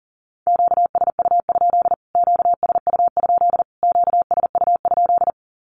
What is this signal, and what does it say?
Morse code, heard as a keyed tone, about 40 wpm, 700 Hz. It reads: QSU? QSU? QSU?
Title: Q-Codes Repeated 3x 40wpm